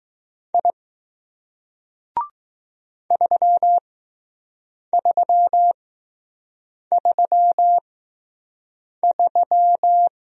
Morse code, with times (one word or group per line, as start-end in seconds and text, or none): 0.5-0.7 I
3.1-10.1 3333